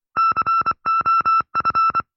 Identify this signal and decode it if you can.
Morse code, heard as a keyed tone, about 25 wpm, 1340 Hz. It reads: COF